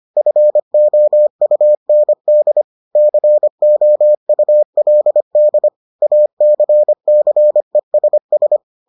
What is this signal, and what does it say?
Morse code, heard as a keyed tone, about 25 wpm, 600 Hz. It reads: FOUND COULD ACCESS